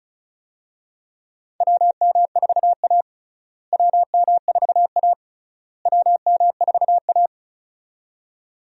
Morse code, heard as a keyed tone, about 36 wpm, 700 Hz. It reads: WM4A WM4A WM4A
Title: USA Call Signs - 4 Characters - Repeated 3x - 35wpm